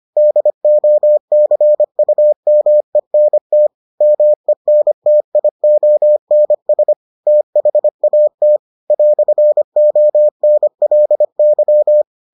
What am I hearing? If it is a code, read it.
Morse code, heard as a keyed tone, about 25 wpm, 600 Hz. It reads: DOCUMENT MENTIONS THAT "ONLY